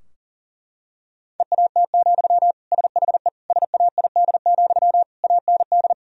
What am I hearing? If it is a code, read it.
Morse code, heard as a keyed tone, about 40 wpm, 700 Hz. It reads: EAT, SHE SAID, AND
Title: Sentences from Top 1000 Words - Review Part 2 - 40wpm